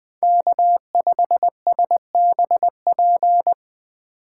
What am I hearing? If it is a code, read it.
Morse code, heard as a keyed tone, about 19 wpm, 700 Hz. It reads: K5SBP